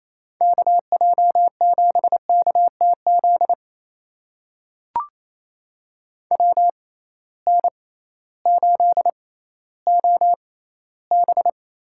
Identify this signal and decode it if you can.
Morse code, heard as a keyed tone, about 28 wpm, spaced as for 15 wpm, 700 Hz. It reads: KJ7KTZ WN8OB
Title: USA Call Signs - 6 Characters - 6x Character Spacing - 28wpm